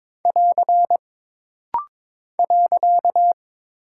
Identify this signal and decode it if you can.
Morse code, heard as a keyed tone, about 22 wpm, 700 Hz. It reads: <AR> .
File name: practice / 22wpm / Single Punctuation-Prosign - Rapid-Fire - 22wpm